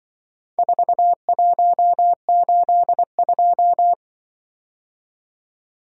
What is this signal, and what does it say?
Morse code, heard as a keyed tone, about 25 wpm, 700 Hz. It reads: 4182
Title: Speed-Racing - Four Letter-Number Combinations - 20wpm